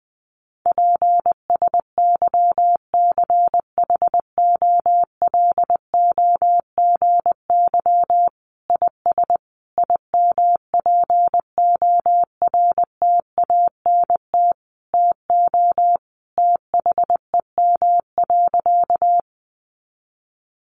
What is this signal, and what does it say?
Morse code, heard as a keyed tone, about 20 wpm, 700 Hz. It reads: PSYCHOLOGY IS IMPORTANT TO THEM.
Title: Sentences from Top 1400 Words 20wpm